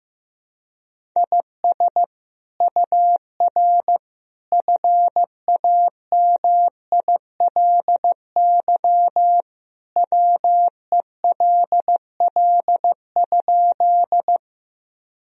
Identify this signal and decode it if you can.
Morse code, heard as a keyed tone, about 15 wpm, 700 Hz. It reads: IS UR FAMILY WELL?
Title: Sentences from Top 400 Words and Abbreviations 15wpm